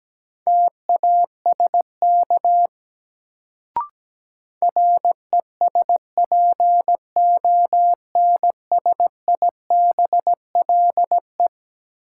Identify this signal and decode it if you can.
Morse code, heard as a keyed tone, about 17 wpm, 700 Hz. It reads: TASK RESPONSIBLE
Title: Sentences from Top 1000 Words - Review Part 7 - New Word Drill - 17wpm